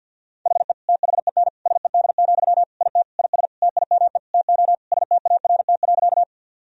Morse code, heard as a keyed tone, about 48 wpm, 700 Hz. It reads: HE THEN SED, IT IS TIME TO START.
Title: Sentences from Top 300 Words and Abbreviations 50wpm